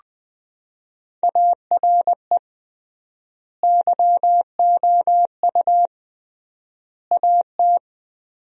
Morse code, heard as a keyed tone, about 20 wpm, 700 Hz. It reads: ARE YOU AT
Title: 3x Word-Spacing - Sentences from Top 200 Words - 20wpm